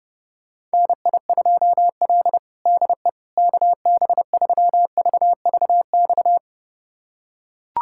Morse code, heard as a keyed tone, about 30 wpm, 700 Hz. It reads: NI2L DE KB3VVX
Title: USA Callsign DE Callsign 30wpm